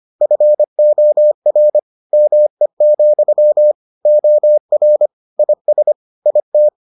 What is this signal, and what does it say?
Morse code, heard as a keyed tone, about 25 wpm, 600 Hz. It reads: FOR ME, OR IS IT